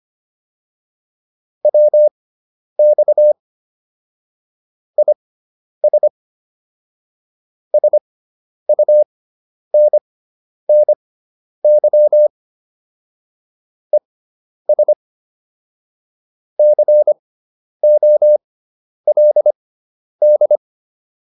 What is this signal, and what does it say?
Morse code, heard as a keyed tone, about 25 wpm, spaced as for 10 wpm, 600 Hz. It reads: WX IS SUNNY ES COLD
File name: QSO310 10